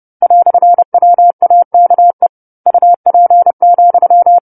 Morse code, heard as a keyed tone, about 30 wpm, 700 Hz. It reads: "WAKE UP,
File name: 10-Chapter-10 30 WPM